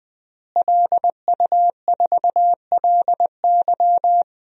Morse code, heard as a keyed tone, about 20 wpm, 700 Hz. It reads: LU4LY
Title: DXCC-100 Call Signs - 5 Characters - Repeated 3x - 20wpm